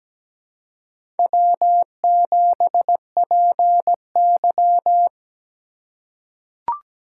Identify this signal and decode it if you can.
Morse code, heard as a keyed tone, about 17 wpm, 700 Hz. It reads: W7PY
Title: USA Call Signs - 2x Character Spacing - 17wpm